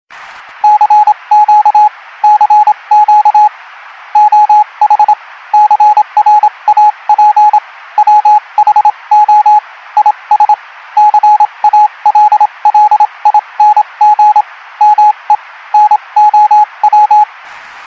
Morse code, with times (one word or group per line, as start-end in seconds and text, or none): 0.6-1.9 CQ
2.2-3.5 CQ
4.2-5.1 OH
5.5-7.6 CRAP
8.0-9.6 WHO
10.0-10.5 IS
11.0-14.4 CALLING
14.8-15.3 ME
15.7-17.2 NOW